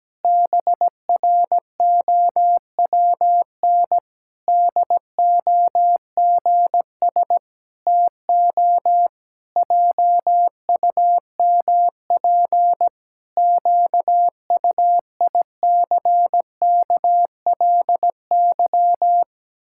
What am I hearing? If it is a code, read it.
Morse code, heard as a keyed tone, about 17 wpm, 700 Hz. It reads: BROWN DOGS TO JUMP QUICKLY
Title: Pangram Challenge v2 - 17wpm